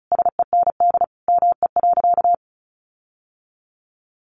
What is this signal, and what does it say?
Morse code, heard as a keyed tone, about 34 wpm, 700 Hz. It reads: SEND ME.
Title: Sentences from Top 1500 Words 35wpm